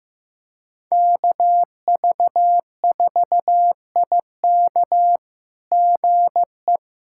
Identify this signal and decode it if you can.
Morse code, heard as a keyed tone, about 15 wpm, 700 Hz. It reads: KV4IK GE
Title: SOTA QSO Practice 15wpm